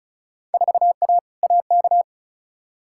Morse code, heard as a keyed tone, about 36 wpm, 700 Hz. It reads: 4A AK
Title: Field Day Exchanges Encore - 35wpm